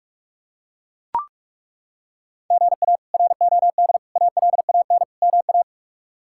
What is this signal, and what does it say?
Morse code, heard as a keyed tone, about 43 wpm, 700 Hz. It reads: GA ROD ALAN MA
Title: K1USN SST QSO Practice - 45wpm